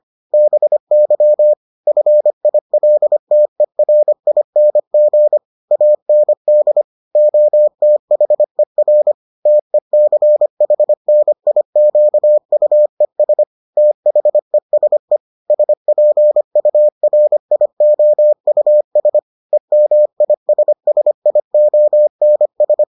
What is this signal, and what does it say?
Morse code, heard as a keyed tone, about 25 wpm, 600 Hz. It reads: BY FILTERING AND OTHER TECHNIQUES THESE SPURIOUS EMISSIONS